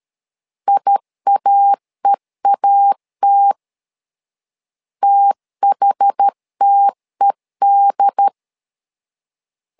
Morse code, heard as a keyed tone, about 13 wpm, 800 Hz. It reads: IAEAT THTED
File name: LEZIONI / 2 UHDN / 2 UHD 12wpm 1540